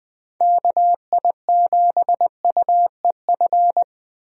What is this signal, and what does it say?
Morse code, heard as a keyed tone, about 20 wpm, 700 Hz. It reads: KI7UEF
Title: USA Call Signs - 6 Characters - Repeated 3x - 20wpm